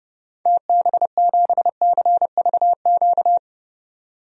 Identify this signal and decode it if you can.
Morse code, heard as a keyed tone, about 30 wpm, 700 Hz. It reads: TB7CVQ
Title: DXCC-100 Call Signs - 6 Characters - 30wpm